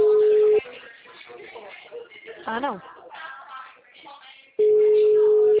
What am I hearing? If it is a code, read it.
Ringback tone (Europe, 425 Hz), with speech.
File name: out- -2003- - - .4144